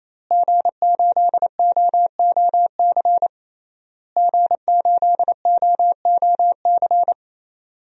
Morse code, heard as a keyed tone, about 28 wpm, 700 Hz. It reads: G8OOC G8OOC
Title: DXCC-100 Call Signs - 5 Characters - Repeated 3x - 28wpm